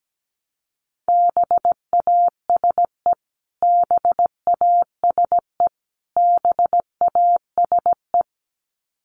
Morse code, heard as a keyed tone, about 17 wpm, 700 Hz. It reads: BASE BASE BASE
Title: Top 500 Words Repeated 17wpm